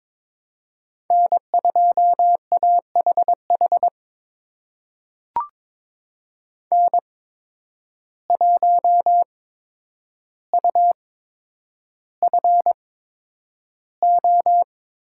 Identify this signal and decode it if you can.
Morse code, heard as a keyed tone, about 21 wpm, spaced as for 9 wpm, 700 Hz. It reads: N2AHH N1UFO
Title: USA Call Signs - 5 Characters - 8x Character Spacing - 22wpm